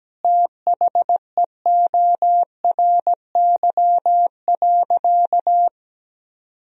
Morse code, heard as a keyed tone, about 17 wpm, 700 Hz. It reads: THEORY.
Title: Sentences from Top 1000 Words - Review Part 3 - 17wpm